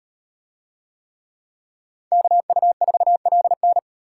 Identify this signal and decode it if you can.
Morse code, heard as a keyed tone, about 34 wpm, 700 Hz. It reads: KU4LN